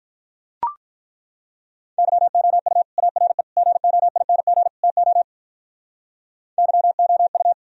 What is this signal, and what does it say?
Morse code, heard as a keyed tone, about 42 wpm, 700 Hz. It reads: YOU ARE GOING TO YOU
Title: Speed-Racing - Sets of 4 Words 35wpm